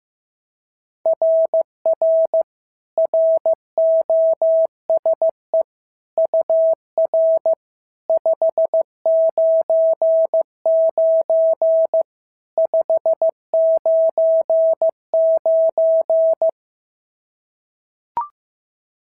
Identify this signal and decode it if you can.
Morse code, heard as a keyed tone, about 15 wpm, 650 Hz. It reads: RR ROSE UR 599 599
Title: SOTA QSO Practice 15wpm